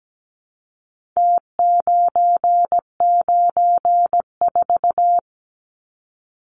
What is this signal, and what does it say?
Morse code, heard as a keyed tone, about 17 wpm, 700 Hz. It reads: T994
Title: Numbers - Contest Style - 4-Digit 17wpm